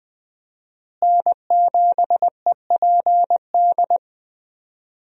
Morse code, heard as a keyed tone, about 20 wpm, 700 Hz. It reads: N7EPD
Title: USA Call Signs - 5 Characters - Repeated 2x - 20wpm